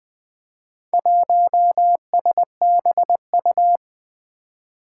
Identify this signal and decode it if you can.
Morse code, heard as a keyed tone, about 21 wpm, 700 Hz. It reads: 1SBU
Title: Four Challenging Letter-Number Combinations 20wpm